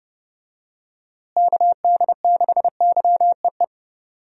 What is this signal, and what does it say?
Morse code, heard as a keyed tone, about 30 wpm, 700 Hz. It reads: KD6YEE
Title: USA Call Signs - 6 Characters - Repeated 3x - 30wpm